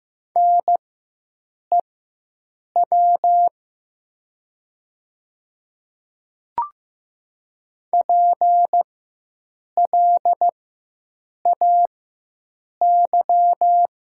Morse code, heard as a keyed tone, about 15 wpm, spaced as for 7 wpm, 700 Hz. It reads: NEW PLAY
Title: 4x Character Spacing - Top 200 Words - 15wpm